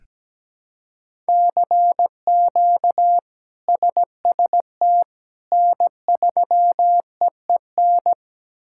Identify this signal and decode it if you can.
Morse code, heard as a keyed tone, about 17 wpm, 700 Hz. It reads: CQ SST N3EEN
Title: K1USN SST QSO Practice - 17wpm